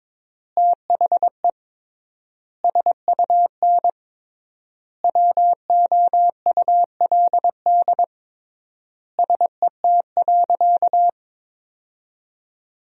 Morse code, heard as a keyed tone, about 22 wpm, 700 Hz. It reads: THE SUN WOULD SET.
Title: 3x Word-Spacing - Sentences from Top 400 Words - 22wpm